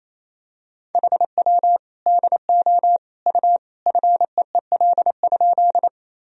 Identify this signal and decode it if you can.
Morse code, heard as a keyed tone, about 28 wpm, 700 Hz. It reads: HW DO U FEEL?
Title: Sentences from Top 400 Words and Abbreviations 28wpm